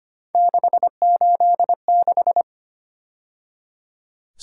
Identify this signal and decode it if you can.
Morse code, heard as a keyed tone, about 29 wpm, 700 Hz. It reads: 686
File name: Speed-Racing - Three Letter-Number Combinations - 25wpm